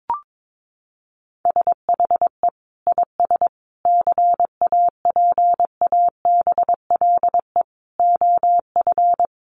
Morse code, heard as a keyed tone, about 22 wpm, 700 Hz. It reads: SHE IS CAPABLE OF
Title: Sentences from Top 1200 Words 22wpm